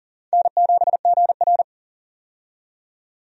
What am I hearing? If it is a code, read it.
Morse code, heard as a keyed tone, about 38 wpm, 700 Hz. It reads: N7GR
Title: USA Call Signs - 4 Characters - 40wpm